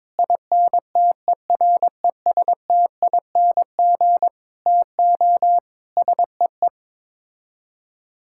Morse code, heard as a keyed tone, about 22 wpm, 700 Hz. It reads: INTERESTING TO SEE
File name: practice / 22wpm / Sets of 7 Words - 22wpm